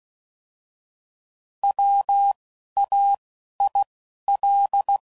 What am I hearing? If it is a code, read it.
Morse code, heard as a keyed tone, about 16 wpm, spaced as for 11 wpm, 800 Hz. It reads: WAIL
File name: Prefixes / By First Letter / Ws